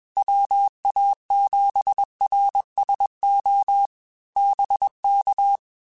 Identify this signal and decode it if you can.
Morse code, heard as a keyed tone, about 21 wpm, 800 Hz. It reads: WA7RSO BK
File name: QSOs / QSOs-mp3s / WD0BGP 1@20